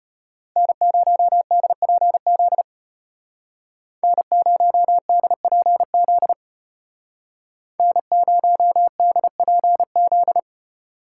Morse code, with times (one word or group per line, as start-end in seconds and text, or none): 0.6-2.6 N0DPZ
4.0-6.3 N0DPZ
7.8-10.4 N0DPZ